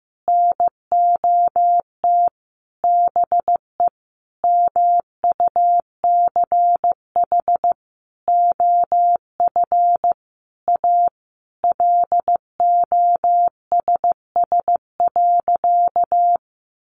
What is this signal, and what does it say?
Morse code, heard as a keyed tone, about 15 wpm, 700 Hz. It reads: NOT BE MUCH OF A LOSS.
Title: Sentences from Top 1400 Words 15wpm